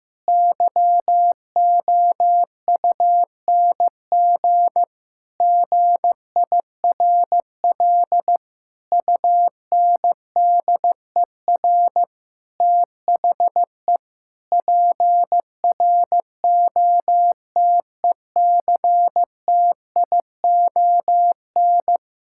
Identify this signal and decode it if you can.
Morse code, heard as a keyed tone, about 15 wpm, 700 Hz. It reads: YOUNG GIRL UNDER THE PROTECTION
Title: Sentences from Top 1600 Words 15wpm